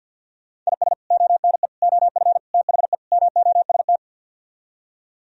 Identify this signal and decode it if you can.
Morse code, heard as a keyed tone, about 64 wpm, spaced as for 56 wpm, 700 Hz. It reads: IS ONE OF THE MOST
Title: Speed-Racing - Sets of 5 Words 50wpm